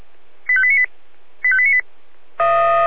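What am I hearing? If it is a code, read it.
Five-tone selective call (ZVEI), heard as quick runs of five tones, tones of about 70 ms. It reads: 85898 85898
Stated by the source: Five-tone ZVEI selective call 85310 received by SDR, from audio.f85360.feuerwehr-orlamuende.org fire brigade server.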